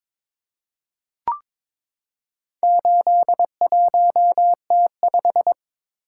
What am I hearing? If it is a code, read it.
Morse code, heard as a keyed tone, about 22 wpm, 700 Hz. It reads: 81T5
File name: Four Letter-Number Combinations - 22wpm